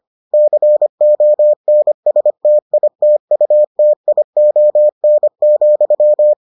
Morse code, heard as a keyed tone, about 25 wpm, 600 Hz. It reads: CONSTITUTION,